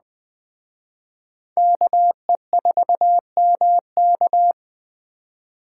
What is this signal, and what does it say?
Morse code, heard as a keyed tone, about 19 wpm, 700 Hz. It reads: KE4MK